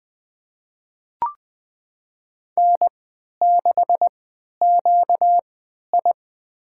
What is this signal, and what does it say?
Morse code, heard as a keyed tone, about 20 wpm, spaced as for 11 wpm, 700 Hz. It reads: N6QI
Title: USA Call Signs - 4 Characters - 3x Character Spacing - 20wpm